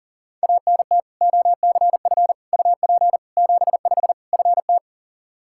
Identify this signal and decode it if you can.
Morse code, heard as a keyed tone, about 40 wpm, 700 Hz. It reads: ANT OCF UP 75 FT
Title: Practice QSO - Exchange 2 - Info - 40wpm